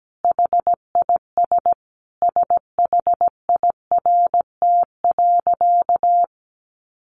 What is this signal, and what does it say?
Morse code, heard as a keyed tone, about 17 wpm, 700 Hz. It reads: HIS SHIRT.
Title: A Sound Of Thunder 17wpm